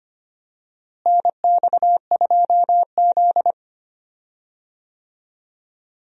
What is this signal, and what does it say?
Morse code, heard as a keyed tone, about 25 wpm, 700 Hz. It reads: NX2Z
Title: USA Call Signs - 4 Characters - Repeated 3x - 25wpm